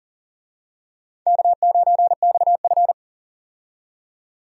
Morse code, heard as a keyed tone, about 37 wpm, 700 Hz. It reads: K9XF
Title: USA Call Signs - 4 Characters - 40wpm